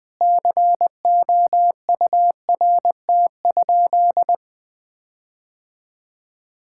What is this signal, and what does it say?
Morse code, heard as a keyed tone, about 21 wpm, 700 Hz. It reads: COURT?
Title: Sentences from Top 1600 Words 20wpm